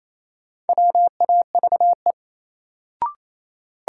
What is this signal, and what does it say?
Morse code, heard as a keyed tone, about 28 wpm, 700 Hz. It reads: WAVE
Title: Sentences from Top 1400 Words - New Word Drill Alt - 28wpm